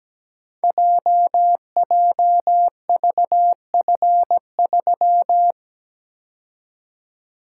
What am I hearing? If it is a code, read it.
Morse code, heard as a keyed tone, about 17 wpm, 690 Hz. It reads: JJVF3
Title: Code Groups - 5 Letter:Numbers - 17wpm